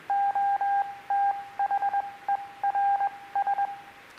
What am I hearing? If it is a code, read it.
Morse code, heard as a keyed tone, about 21 wpm, 800 Hz, with radio noise. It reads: O T H E R S